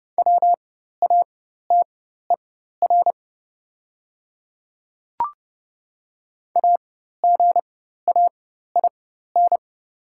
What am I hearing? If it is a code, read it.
Morse code, heard as a keyed tone, about 30 wpm, spaced as for 14 wpm, 700 Hz. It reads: WATER AGAIN
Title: 4x Character Spacing - Top 300 Words - 30wpm